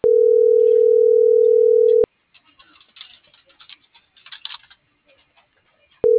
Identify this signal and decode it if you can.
Ringback tone (North America), 440 + 480 Hz.